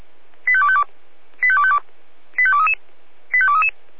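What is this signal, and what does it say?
Five-tone selective call (ZVEI), heard as quick runs of five tones, tones of about 70 ms. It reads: 85252 85252 85233 85233
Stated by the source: ZVEI five-tone selective call 85250 received with an SDR.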